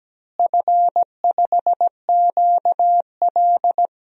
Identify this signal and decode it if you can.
Morse code, heard as a keyed tone, about 17 wpm, 700 Hz. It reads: F5QL
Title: DXCC-100 Call Signs - 4 Characters - Repeated 3x - 17wpm